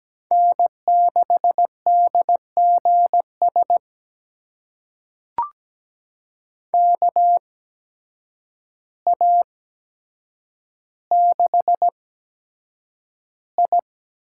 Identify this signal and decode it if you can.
Morse code, heard as a keyed tone, about 17 wpm, spaced as for 8 wpm, 700 Hz. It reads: N6DGS KA6I
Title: USA Call Signs - 8x Character Spacing - 17wpm